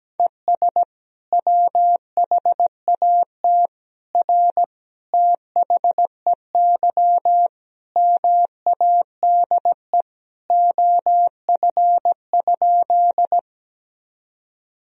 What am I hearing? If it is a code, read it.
Morse code, heard as a keyed tone, about 17 wpm, 700 Hz. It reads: ES WHAT R THEY MADE OF?
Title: Sentences from Top 200 Words and Abbreviations 17wpm